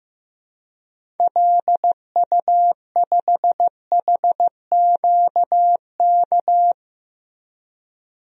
Morse code, heard as a keyed tone, about 16 wpm, 700 Hz. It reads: LU5HQK
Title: DXCC-100 Call Signs - 6 Characters - Repeated 2x - 15wpm